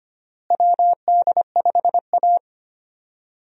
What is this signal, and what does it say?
Morse code, heard as a keyed tone, about 25 wpm, 700 Hz. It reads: WD5A
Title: USA Call Signs - 4 Characters - Repeated 3x - 25wpm